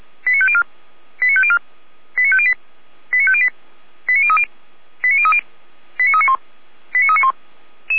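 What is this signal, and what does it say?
Five-tone selective call (ZVEI), heard as quick runs of five tones, tones of about 70 ms. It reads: 89504 89504 89508 89508 89030 89030 89391 89391
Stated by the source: ZVEI five-tone selective call 89140, received via SDR.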